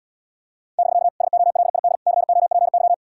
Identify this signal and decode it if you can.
Morse code, heard as a keyed tone, about 78 wpm, 700 Hz. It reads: <BK> SOLID COPY